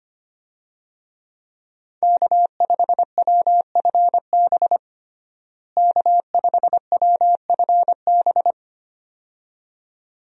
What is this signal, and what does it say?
Morse code, heard as a keyed tone, about 25 wpm, 700 Hz. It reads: K5WFB K5WFB
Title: USA Call Signs - 5 Characters - Repeated 2x - 25wpm